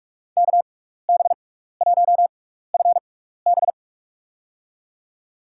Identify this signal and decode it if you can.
Morse code, heard as a keyed tone, about 46 wpm, spaced as for 16 wpm, 700 Hz. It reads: KB1FB
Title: USA Call Signs - 5 Characters - 6x Character Spacing - 45wpm